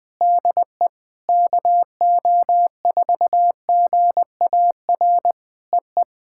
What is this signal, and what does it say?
Morse code, heard as a keyed tone, about 20 wpm, 700 Hz. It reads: DE KO4GAR EE
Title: POTA QSO Practice - 20wpm